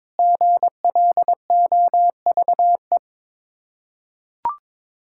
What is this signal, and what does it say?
Morse code, heard as a keyed tone, about 22 wpm, 700 Hz. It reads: GLOVE
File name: Difficult Words with Hints - 22wpm